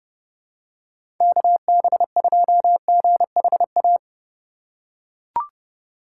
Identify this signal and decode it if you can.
Morse code, heard as a keyed tone, about 30 wpm, 700 Hz. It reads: KB2GHA
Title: USA Call Signs - 6 Characters - 2x Character Spacing - 30wpm